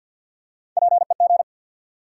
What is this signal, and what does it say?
Morse code, heard as a keyed tone, about 52 wpm, 700 Hz. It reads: PEG
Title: USA Common Names - Max 5 Letters - 50wpm